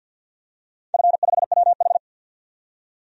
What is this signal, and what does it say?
Morse code, heard as a keyed tone, about 47 wpm, 700 Hz. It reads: F5WH
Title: DXCC-100 Call Signs - 4 Characters - 50wpm